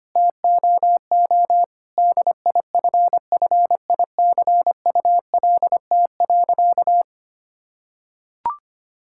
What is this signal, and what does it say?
Morse code, heard as a keyed tone, about 25 wpm, 700 Hz. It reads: TOO DIFFICULT.
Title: Sentences from Top 1000 Words - Review Part 5 - 25wpm